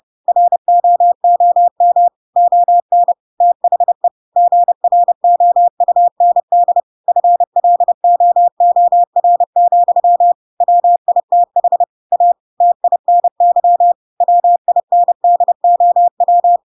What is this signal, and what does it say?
Morse code, heard as a keyed tone, about 30 wpm, 700 Hz. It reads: ROOM ON THE GROUND FLOOR, WITH A TINY WINDOW
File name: Pinocchio / Pinocchio ch3 30 30 ik1hge